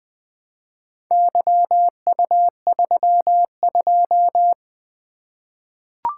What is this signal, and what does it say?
Morse code, heard as a keyed tone, about 20 wpm, 700 Hz. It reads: YU32
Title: Four Letter-Number Combinations - 20wpm